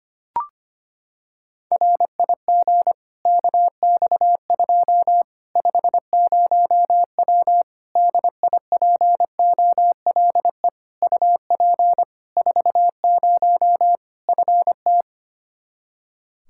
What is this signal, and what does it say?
Morse code, heard as a keyed tone, about 25 wpm, 700 Hz. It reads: RIG KX2 50W DIPOLE UP 40 FT
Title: Practice QSO - Exchange 2 - Info - 25wpm